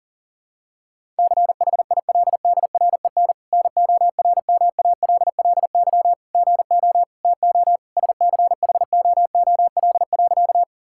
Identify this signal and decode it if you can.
Morse code, heard as a keyed tone, about 40 wpm, 700 Hz. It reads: CHILDREN NORMALLY GO TO SCHOOL.
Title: Sentences from Top 1000 Words - Review Part 6 - 40wpm